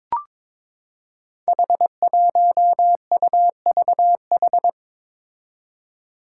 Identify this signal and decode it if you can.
Morse code, heard as a keyed tone, about 22 wpm, 700 Hz. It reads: H1UVH
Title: Five Challenging Letter-Number Combinations 22wpm